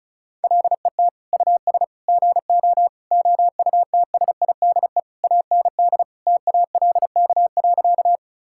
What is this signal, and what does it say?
Morse code, heard as a keyed tone, about 36 wpm, 700 Hz. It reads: LET US GO OUTSIDE AND TALK.
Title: Sentences from Top 1000 Words - Review Part 6 - 35wpm